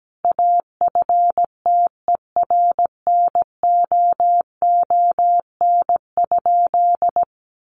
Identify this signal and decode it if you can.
Morse code, heard as a keyed tone, about 17 wpm, 700 Hz. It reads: AFTERNOON?
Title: Sentences from Top 1500 Words 17wpm